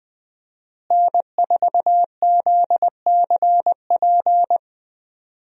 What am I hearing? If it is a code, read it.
Morse code, heard as a keyed tone, about 20 wpm, 700 Hz. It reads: N4ZCP
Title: USA Call Signs - 5 Characters - Repeated 2x - 20wpm